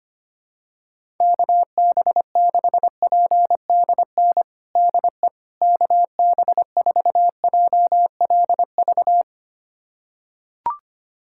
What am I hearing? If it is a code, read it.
Morse code, heard as a keyed tone, about 25 wpm, 700 Hz. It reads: KB6PDN DE KB4JLV
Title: USA Callsign DE Callsign - 6 Characters - 25wpm